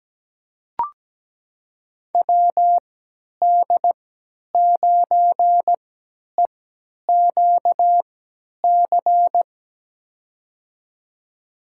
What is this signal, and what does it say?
Morse code, heard as a keyed tone, about 17 wpm, spaced as for 10 wpm, 700 Hz. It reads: WD9EQC
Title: USA Call Signs - 6 Characters - 3x Character Spacing - 17wpm